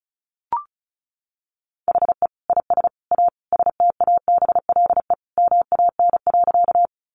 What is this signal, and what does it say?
Morse code, heard as a keyed tone, about 34 wpm, 700 Hz. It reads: HE IS A STABLE MAN.
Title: Sentences from Top 1500 Words 35wpm